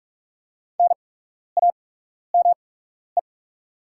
Morse code, heard as a keyed tone, about 44 wpm, spaced as for 12 wpm, 690 Hz. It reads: NAME
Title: 8x Character Spacing - Top 200 Words - 45wpm